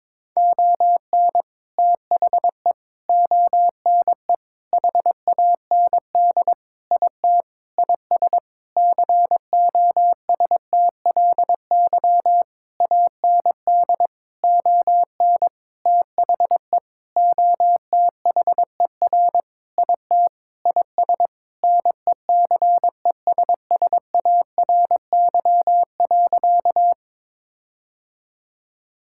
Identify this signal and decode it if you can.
Morse code, heard as a keyed tone, about 21 wpm, 700 Hz. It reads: ON THE ONE HAND IT IS COSTLY AND ON THE OTHER IT IS NECESSARY.
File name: practice / 22wpm / Sentences with 8-Word Phrases - 22wpm